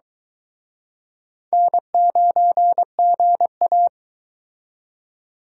Morse code, heard as a keyed tone, about 22 wpm, 700 Hz. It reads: N9GA